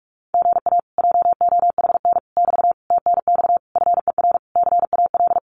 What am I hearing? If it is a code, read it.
Morse code, heard as a keyed tone, about 46 wpm, 700 Hz. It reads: GA JOHN <BT> TNX FER CAL
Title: Common QSO Phrases Encore 45wpm